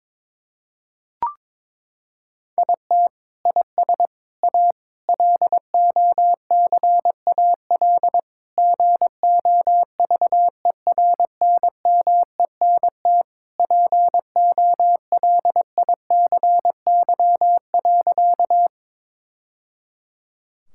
Morse code, heard as a keyed tone, about 22 wpm, 700 Hz. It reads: IT IS A LOCAL GOVERNMENT POLICY.
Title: Sentences from Top 1000 Words - Review Part 5 - 22wpm